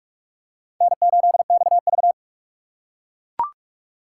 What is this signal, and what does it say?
Morse code, heard as a keyed tone, about 39 wpm, 700 Hz. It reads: N8XV